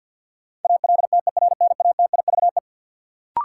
Morse code, heard as a keyed tone, about 52 wpm, 700 Hz. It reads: ALTERNATIVE
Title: Sentences from Top 1000 Words - Review Part 4 - New Word Drill - 50wpm